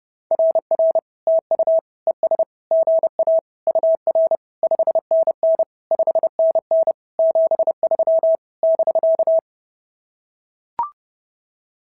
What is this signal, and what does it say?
Morse code, heard as a keyed tone, about 30 wpm, 690 Hz. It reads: RR TU ES GA UR 5NN 5NN 73 <BK>
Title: SOTA QSO Practice 30wpm